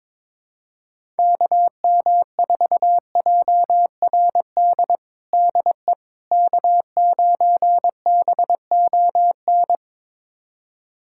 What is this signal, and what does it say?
Morse code, heard as a keyed tone, about 22 wpm, 700 Hz. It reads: KM4JRD DE K9BON
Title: USA Callsign DE Callsign 22wpm